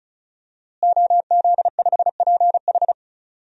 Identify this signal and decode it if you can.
Morse code, heard as a keyed tone, about 36 wpm, 700 Hz. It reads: OZ5PH
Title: DXCC-100 Call Signs - 5 Characters - Repeated 3x - 35wpm